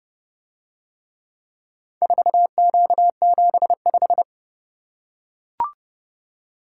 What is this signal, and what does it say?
Morse code, heard as a keyed tone, about 28 wpm, 700 Hz. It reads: 4Q75